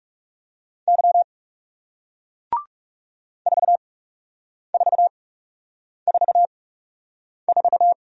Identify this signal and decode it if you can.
Morse code, heard as a keyed tone, about 38 wpm, spaced as for 8 wpm, 700 Hz. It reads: Y 4444
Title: Speed-Racing Single Letter-Number 30wpm